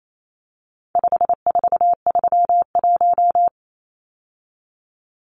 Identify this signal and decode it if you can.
Morse code, heard as a keyed tone, about 28 wpm, 700 Hz. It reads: 5431
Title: Numbers - 4-Digit 28wpm